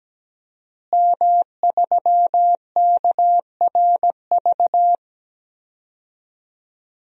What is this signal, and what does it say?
Morse code, heard as a keyed tone, about 17 wpm, 700 Hz. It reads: M3KRV
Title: DXCC-100 Call Signs - 5 Characters - Repeated 2x - 17wpm